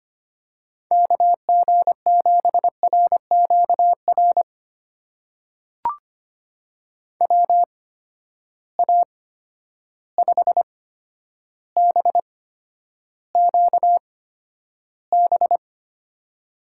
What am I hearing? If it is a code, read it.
Morse code, heard as a keyed tone, about 25 wpm, spaced as for 11 wpm, 700 Hz. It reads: KG7RQR WA5BQB
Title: USA Call Signs - 6 Characters - 8x Character Spacing - 25wpm